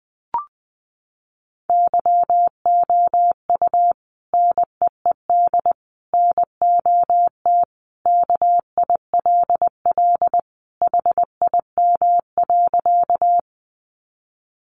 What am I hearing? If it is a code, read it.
Morse code, heard as a keyed tone, about 20 wpm, 700 Hz. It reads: YOU NEED NOT KILL HIM.
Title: Sentences from Top 1400 Words 20wpm